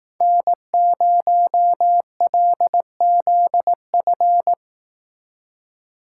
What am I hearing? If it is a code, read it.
Morse code, heard as a keyed tone, about 18 wpm, 700 Hz. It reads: N0LZF
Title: Speed Racing - USA 1x3 Call Signs 15wpm